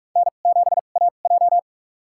Morse code, heard as a keyed tone, about 46 wpm, spaced as for 33 wpm, 700 Hz. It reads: N7AJ
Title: USA Call Signs - 4 Characters - 2x Character Spacing - 45wpm